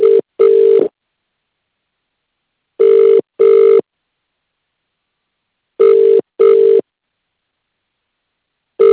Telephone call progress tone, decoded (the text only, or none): ringback tone